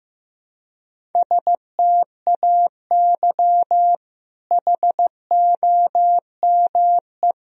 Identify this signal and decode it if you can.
Morse code, heard as a keyed tone, about 15 wpm, 700 Hz. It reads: STAY HOME
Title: Sentences from Top 1000 Words - Review Part 1 - 15wpm